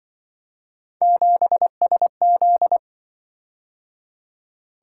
Morse code, heard as a keyed tone, about 26 wpm, 700 Hz. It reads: 7SZ